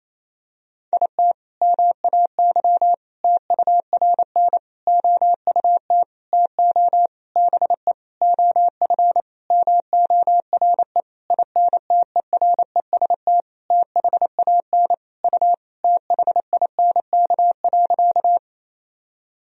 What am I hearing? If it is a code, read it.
Morse code, heard as a keyed tone, about 28 wpm, 700 Hz. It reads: IT MAY TURN OUT TO BE OF MORE INTEREST THAN U THINK.
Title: Sentences from Top 400 Words and Abbreviations 28wpm